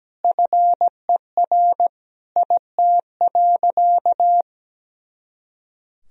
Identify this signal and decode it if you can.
Morse code, heard as a keyed tone, about 17 wpm, 700 Hz. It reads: FER IT.
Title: Sentences from Top 300 Words and Abbreviations 17wpm